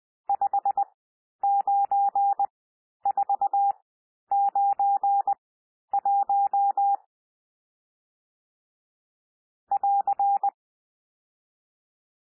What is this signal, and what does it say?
Morse code, heard as a keyed tone, about 20 wpm, spaced as for 11 wpm, 800 Hz. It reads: 59491 <AR>
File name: cwmp3 / N1 SOUND NUMBERS 20 wpm